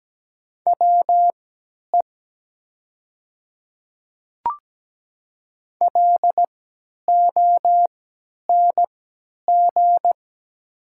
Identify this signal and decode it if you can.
Morse code, heard as a keyed tone, about 17 wpm, spaced as for 10 wpm, 700 Hz. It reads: WE LONG
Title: 3x Character Spacing - Top 100 Words - 17wpm